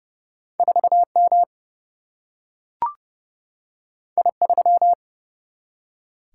Morse code, heard as a keyed tone, about 30 wpm, 700 Hz. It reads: 4 M I 3
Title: Single-Letter Single-Number 30wpm